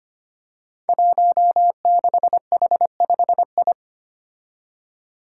Morse code, heard as a keyed tone, about 25 wpm, 700 Hz. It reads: 16H5I
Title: Five Challenging Letter-Number Combinations 25wpm